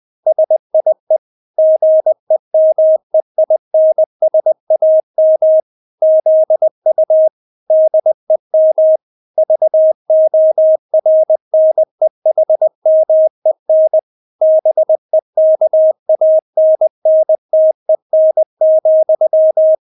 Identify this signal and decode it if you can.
Morse code, heard as a keyed tone, about 20 wpm, 630 Hz. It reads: SIE GEMEINSAM ZU DEM VORNEHMEN BEKANNTEN,
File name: Witz1 20wpm